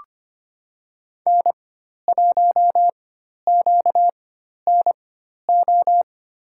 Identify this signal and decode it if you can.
Morse code, heard as a keyed tone, about 25 wpm, spaced as for 12 wpm, 700 Hz. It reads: N1QNO
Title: USA Call Signs - 4x Character Spacing - 25wpm